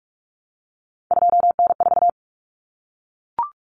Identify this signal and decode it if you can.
Morse code, heard as a keyed tone, about 44 wpm, 700 Hz. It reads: 2N4
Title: Common Cut Numbers - 3-Digit 45wpm